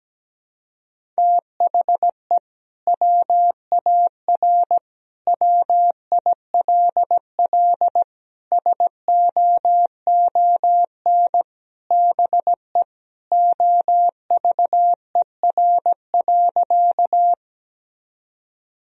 Morse code, heard as a keyed tone, about 17 wpm, 700 Hz. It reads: THE WAR WILL SOON BE OVER.
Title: Sentences from Top 1000 Words - Review Part 2 - 17wpm